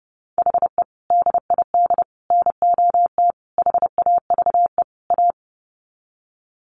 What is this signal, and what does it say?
Morse code, heard as a keyed tone, about 28 wpm, 700 Hz. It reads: HE DID NOT HAVE A
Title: Sets of 5 Words Encore 30wpm